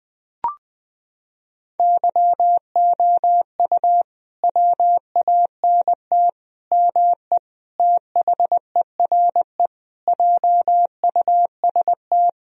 Morse code, heard as a keyed tone, about 21 wpm, 700 Hz. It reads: YOU WANT ME THERE JUST
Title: Sentences from Top 1000 Words - Review Part 2 - 20wpm